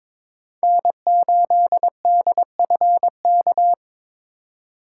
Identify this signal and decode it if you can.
Morse code, heard as a keyed tone, about 26 wpm, 700 Hz. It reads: N8DFK
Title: Speed Racing - USA 1x3 Call Signs 22wpm